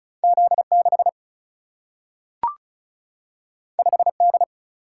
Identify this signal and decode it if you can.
Morse code, heard as a keyed tone, about 35 wpm, 700 Hz. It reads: Z6 5D
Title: Single-Letter Single-Number 35wpm